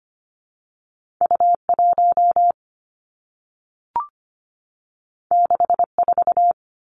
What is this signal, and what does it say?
Morse code, heard as a keyed tone, about 25 wpm, 700 Hz. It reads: U 1 6 4
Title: Two Challenging Letter-Number Combinations 25wpm